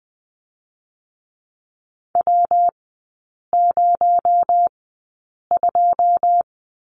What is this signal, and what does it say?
Morse code, heard as a keyed tone, about 20 wpm, spaced as for 8 wpm, 700 Hz. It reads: W02
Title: 218 Lesson 35e - Number 0 - Three Characters 20wpm